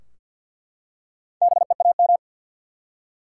Morse code, heard as a keyed tone, about 50 wpm, 700 Hz. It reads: BEAM
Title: Top 90 QSO Elements 50wpm